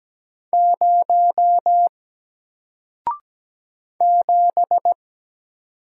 Morse code, heard as a keyed tone, about 17 wpm, 700 Hz. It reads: 0 7
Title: Single Letter-Number Alt 17wpm